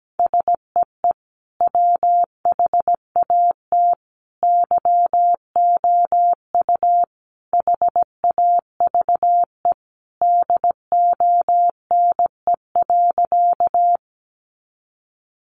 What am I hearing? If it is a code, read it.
Morse code, heard as a keyed tone, about 17 wpm, 700 Hz. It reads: SEE WHAT YOU HAVE DONE.
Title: Sentences from Top 500 Words 17wpm